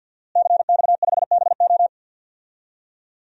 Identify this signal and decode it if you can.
Morse code, heard as a keyed tone, about 51 wpm, 700 Hz. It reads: CX5BO
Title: DXCC-100 Call Signs - 5 Characters - Repeated 3x - 50wpm